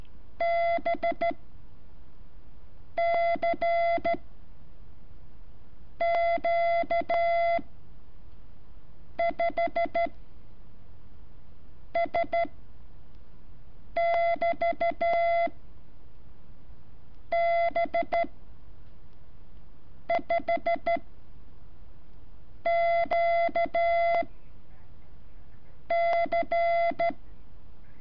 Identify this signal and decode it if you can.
Morse code, heard as a keyed tone, about 13 wpm, spaced as for 4 wpm, 680 Hz. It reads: BCQ5S<BT>B5QC
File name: curso / ejercicio 17